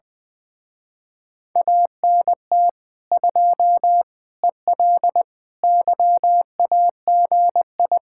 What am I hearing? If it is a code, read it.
Morse code, heard as a keyed tone, about 21 wpm, 700 Hz. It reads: ANT 2 EL YAGI
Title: Speed-Racing Common QSO Phrases Encore - 20wpm